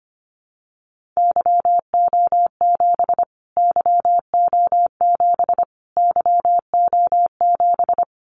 Morse code, heard as a keyed tone, about 25 wpm, 700 Hz. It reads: YO7 YO7 YO7
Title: Callsign Prefixes for Top 25 DXCC - Repeated 3x - 25wpm